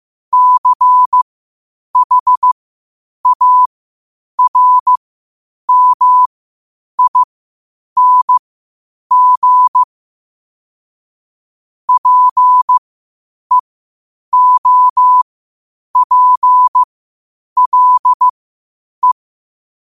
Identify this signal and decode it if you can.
Morse code, heard as a keyed tone, about 15 wpm, spaced as for 8 wpm, 1000 Hz. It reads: CHARMING PEOPLE
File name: qotdfiles / 8wpm